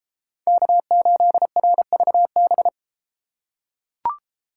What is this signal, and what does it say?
Morse code, heard as a keyed tone, about 31 wpm, 700 Hz. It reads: K8RVB